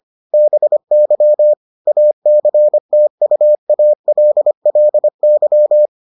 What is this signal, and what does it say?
Morse code, heard as a keyed tone, about 25 wpm, 600 Hz. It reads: BY ACTUALLY